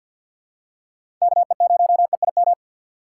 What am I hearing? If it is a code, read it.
Morse code, heard as a keyed tone, about 51 wpm, 700 Hz. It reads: KE0EIM
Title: USA Call Signs - 6 Characters - 4x Character Spacing - 50wpm